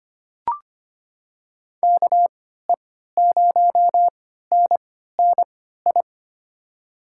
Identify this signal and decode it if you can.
Morse code, heard as a keyed tone, about 25 wpm, spaced as for 14 wpm, 700 Hz. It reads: KE0NNI
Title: USA Call Signs - 6 Characters - 3x Character Spacing - 25wpm